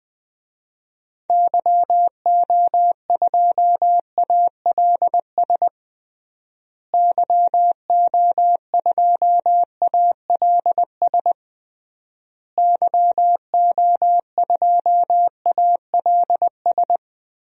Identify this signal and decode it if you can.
Morse code, heard as a keyed tone, about 20 wpm, 700 Hz. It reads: YO2ALS YO2ALS YO2ALS
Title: DXCC-100 Call Signs - 6 Characters - Repeated 3x - 20wpm